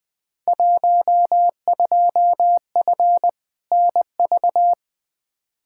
Morse code, heard as a keyed tone, about 20 wpm, 700 Hz. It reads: 12F NV
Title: Field Day Exchanges Encore - 20wpm